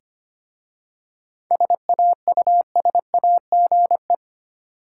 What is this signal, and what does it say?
Morse code, heard as a keyed tone, about 25 wpm, 700 Hz. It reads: SAUSAGE
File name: practice / 25wpm / Difficult Words with Hints - 25wpm